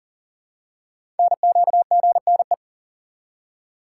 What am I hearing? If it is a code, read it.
Morse code, heard as a keyed tone, about 41 wpm, 700 Hz. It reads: NQGNE
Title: Code Groups - 5 Letters - 40wpm